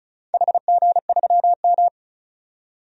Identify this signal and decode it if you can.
Morse code, heard as a keyed tone, about 36 wpm, 700 Hz. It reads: HG3M
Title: DXCC-100 Call Signs - 4 Characters - 35wpm